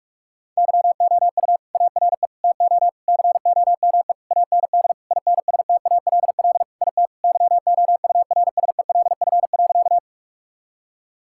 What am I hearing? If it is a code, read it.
Morse code, heard as a keyed tone, about 41 wpm, 700 Hz. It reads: YOU ARE TO COME AND INSTALL IT YOURSELF.
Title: Sentences from Top 1600 Words 45wpm